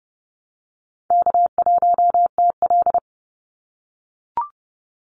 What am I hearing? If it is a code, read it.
Morse code, heard as a keyed tone, about 30 wpm, 700 Hz. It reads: K1TL
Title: Easy Call Signs - 30wpm